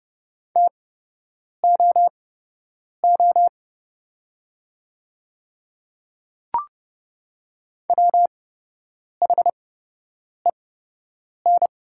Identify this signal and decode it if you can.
Morse code, heard as a keyed tone, about 30 wpm, spaced as for 8 wpm, 700 Hz. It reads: TOO WHEN
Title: 8x Character Spacing - Top 200 Words - 30wpm